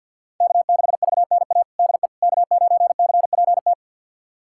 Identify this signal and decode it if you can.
Morse code, heard as a keyed tone, about 51 wpm, 700 Hz. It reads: K6VNA DE K9CPT
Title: USA Callsign DE Callsign - 5 Characters - 50wpm